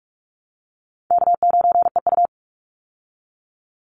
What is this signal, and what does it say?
Morse code, heard as a keyed tone, about 44 wpm, 700 Hz. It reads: K9EU
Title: Call Signs Repeated 3x - 45wpm